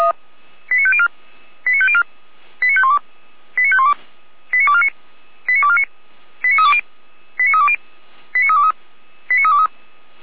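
Five-tone selective call (ZVEI), heard as quick runs of five tones, tones of about 70 ms. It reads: 89504 89504 89512 89512 89249 89249 89320 89320 89323 89323
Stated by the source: ZVEI five-tone selective call received on an SDR.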